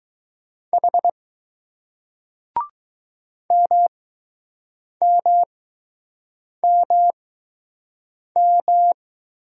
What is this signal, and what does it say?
Morse code, heard as a keyed tone, about 20 wpm, spaced as for 6 wpm, 700 Hz. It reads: H MMMM